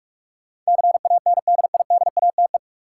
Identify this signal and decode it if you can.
Morse code, heard as a keyed tone, about 46 wpm, 700 Hz. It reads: CANDIDATE
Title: Sentences from Top 1600 Words - New Word Drill - 45wpm